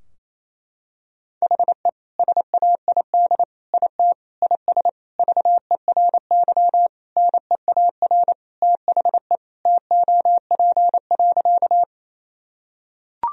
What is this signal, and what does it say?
Morse code, heard as a keyed tone, about 28 wpm, 700 Hz. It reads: HE SAID IT IS VERY NEAR THE TOP.
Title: Sentences from Top 1000 Words - Review Part 2 - 28wpm